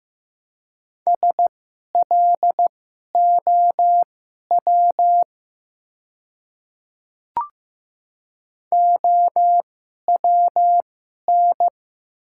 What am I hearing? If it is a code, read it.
Morse code, heard as a keyed tone, about 15 wpm, spaced as for 11 wpm, 700 Hz. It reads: SLOW OWN
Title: 2x Character Spacing - Top 500 Words - 15wpm